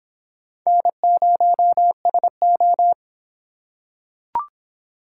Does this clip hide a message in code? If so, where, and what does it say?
Yes: Morse code from 0.7 to 2.9 s, heard as a keyed tone, about 23 wpm, 700 Hz. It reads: N0SO